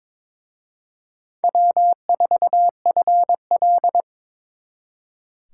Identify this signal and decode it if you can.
Morse code, heard as a keyed tone, about 26 wpm, 700 Hz. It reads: W4FL
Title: Speed Racing - USA 1x2 Call Signs 22wpm